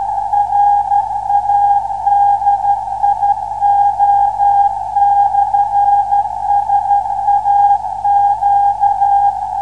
Morse code, heard as a keyed tone, about 12 wpm, 810 Hz. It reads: RADIO/SAQ